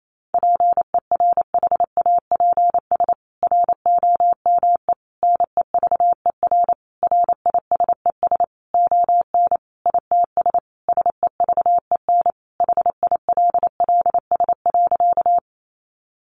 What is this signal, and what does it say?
Morse code, heard as a keyed tone, about 28 wpm, 700 Hz. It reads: PERHAPS ROME NEVER RISES ON ITS SEVEN HILLS.
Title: A Sound Of Thunder 28wpm